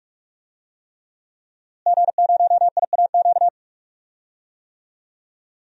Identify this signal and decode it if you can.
Morse code, heard as a keyed tone, about 46 wpm, 700 Hz. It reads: G0IAQ
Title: DXCC-100 Call Signs - 5 Characters - Repeated 3x - 45wpm